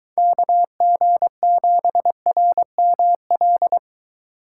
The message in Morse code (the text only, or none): KG7RML